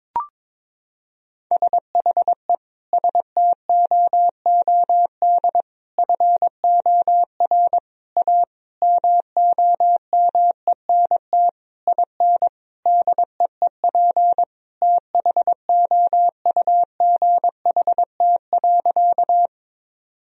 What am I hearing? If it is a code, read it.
Morse code, heard as a keyed tone, about 22 wpm, 700 Hz. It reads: SHE STOOD FOR A MOMENT IN DEEP THOUGHT.
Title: Sentences from Top 1000 Words - Review Part 3 - 22wpm